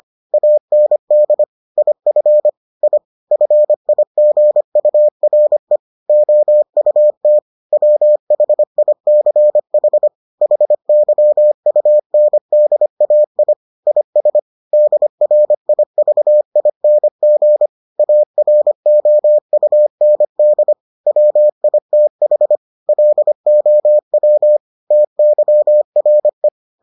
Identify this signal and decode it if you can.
Morse code, heard as a keyed tone, about 25 wpm, 600 Hz. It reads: AND IF I FIGURE OUT WHICH HYUNDAI IS DRIVING AROUND WITH LOW TYRE